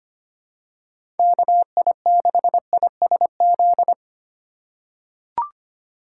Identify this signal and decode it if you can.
Morse code, heard as a keyed tone, about 25 wpm, 700 Hz. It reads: KI6ISZ
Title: USA Call Signs - 6 Characters - 4x Character Spacing - 25wpm